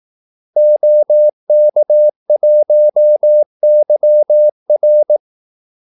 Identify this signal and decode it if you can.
Morse code, heard as a keyed tone, about 18 wpm, 600 Hz. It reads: OK1YR
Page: Kaldesignaler 18 wpm